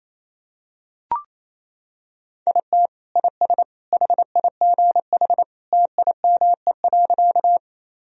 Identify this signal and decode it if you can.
Morse code, heard as a keyed tone, about 28 wpm, 700 Hz. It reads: IT IS HIGH TIME.
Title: Sentences from Top 1000 Words - Review Part 1 - 28wpm